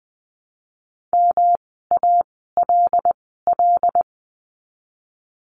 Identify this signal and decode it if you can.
Morse code, heard as a keyed tone, about 20 wpm, spaced as for 15 wpm, 700 Hz. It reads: MALL
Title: 108 Lesson 18f - Letter m - Words 20wpm